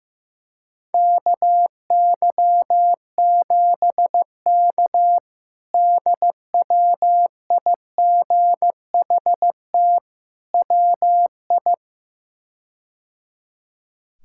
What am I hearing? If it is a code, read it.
Morse code, heard as a keyed tone, about 15 wpm, 700 Hz. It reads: KY7K DWIGHT WI
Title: K1USN SST QSO Practice - 15wpm